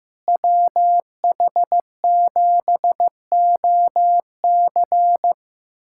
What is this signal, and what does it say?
Morse code, heard as a keyed tone, about 15 wpm, 700 Hz. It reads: WH7OC
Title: USA Callsign DE Callsign - 5 Characters - 15wpm